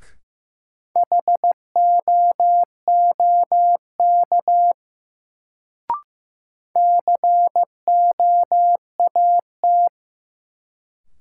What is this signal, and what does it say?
Morse code, heard as a keyed tone, about 15 wpm, 700 Hz. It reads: HOOK COAT
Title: Sentences from Top 1600 Words - New Word Drill Alt - 15wpm